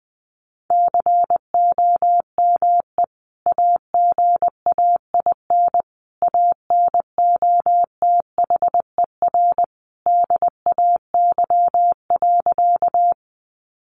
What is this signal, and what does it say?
Morse code, heard as a keyed tone, about 20 wpm, 700 Hz. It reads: COME AGAIN ANOTHER DAY.
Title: Sentences from Top 600 Words 20wpm